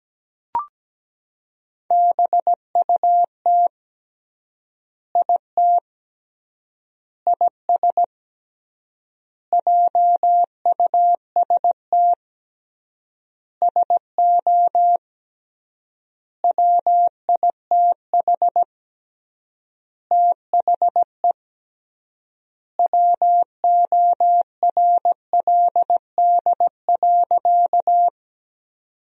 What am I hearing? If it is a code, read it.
Morse code, heard as a keyed tone, about 17 wpm, 700 Hz. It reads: BUT IT IS JUST SO WITH THE WORLD.
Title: 3x Word-Spacing - Sentences from Top 200 Words - 17wpm